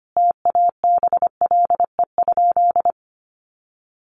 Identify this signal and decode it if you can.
Morse code, heard as a keyed tone, about 25 wpm, 700 Hz. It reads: TABLE?
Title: Sentences from Top 1300 Words 25wpm